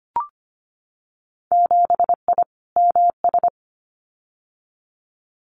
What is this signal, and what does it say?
Morse code, heard as a keyed tone, about 25 wpm, 700 Hz. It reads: 7I MS
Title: Winter Field Day Exchanges Encore 25wpm